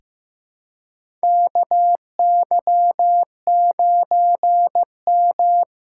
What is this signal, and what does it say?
Morse code, heard as a keyed tone, about 16 wpm, 700 Hz. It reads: KY9M
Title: USA Callsign DE Callsign - 4 Characters - 15wpm